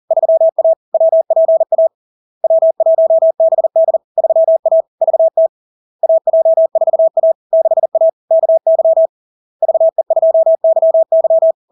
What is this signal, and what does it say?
Morse code, heard as a keyed tone, about 40 wpm, 650 Hz. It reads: 3A WPA W1BD 3A VT AJ4A 6A KY VE2YY
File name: LARK-FD-2022-Log-in-CW-40wpm-650-Hz